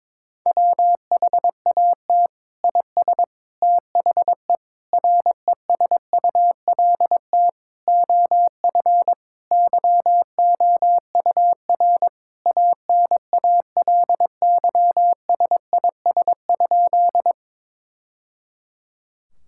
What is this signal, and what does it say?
Morse code, heard as a keyed tone, about 22 wpm, 700 Hz. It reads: WHAT IS THE RESULT OF YOUR ANALYSIS?
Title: Sentences from Top 1000 Words - Review Part 5 - 22wpm